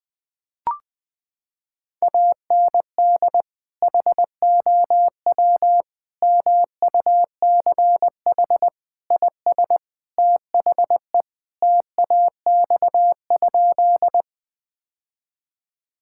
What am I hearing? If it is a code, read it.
Morse code, heard as a keyed tone, about 20 wpm, 700 Hz. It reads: AND HOW MUCH IS THE TAX?
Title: Sentences from Top 1000 Words - Review Part 5 - 20wpm